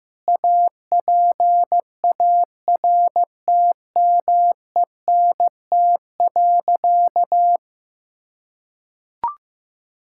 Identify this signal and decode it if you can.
Morse code, heard as a keyed tone, about 16 wpm, 700 Hz. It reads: APARTMENT.
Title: Sentences from Top 1000 Words - Review Part 6 - 15wpm